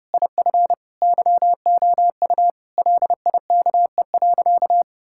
Morse code, heard as a keyed tone, about 30 wpm, 700 Hz. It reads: IF YOU LIKE.
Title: Sentences from Top 1600 Words 30wpm